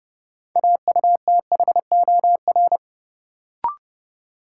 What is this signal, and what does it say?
Morse code, heard as a keyed tone, about 29 wpm, 700 Hz. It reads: AUTHOR
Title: Sentences from Top 1200 Words - New Word Drill Alt - 30wpm